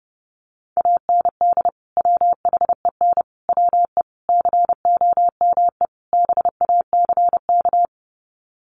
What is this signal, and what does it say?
Morse code, heard as a keyed tone, about 30 wpm, 700 Hz. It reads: AND WHEN WE COME BACK
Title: Sets of 5 Words 30wpm